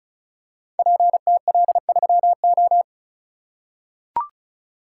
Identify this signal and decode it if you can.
Morse code, heard as a keyed tone, about 34 wpm, 700 Hz. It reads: PTL3O
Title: Code Groups - 5 Letter:Numbers - 35wpm